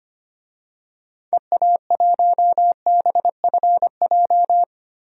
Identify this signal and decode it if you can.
Morse code, heard as a keyed tone, about 25 wpm, 700 Hz. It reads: EA1BFJ
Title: DXCC-100 Call Signs - 6 Characters - 25wpm